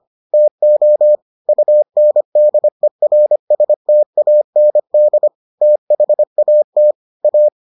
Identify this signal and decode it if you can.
Morse code, heard as a keyed tone, about 25 wpm, 600 Hz. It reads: TO UNDERSTAND THAT A